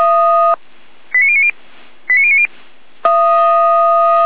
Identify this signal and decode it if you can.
Five-tone selective call (ZVEI), heard as quick runs of five tones, tones of about 70 ms. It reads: 89090 89090